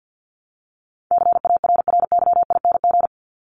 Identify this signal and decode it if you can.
Morse code, heard as a keyed tone, about 50 wpm, 700 Hz. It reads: CARRYING
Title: ING Suffix - Top 100 Words with Base Words 50wpm